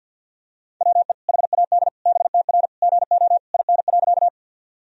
Morse code, heard as a keyed tone, about 50 wpm, 700 Hz. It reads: WE HAD BTR GO IN.
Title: Sentences from Top 400 Words and Abbreviations 50wpm